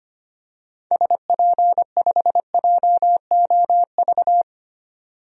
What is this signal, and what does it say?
Morse code, heard as a keyed tone, about 25 wpm, 700 Hz. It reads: SP5JOV
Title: DXCC-100 Call Signs - 6 Characters - 25wpm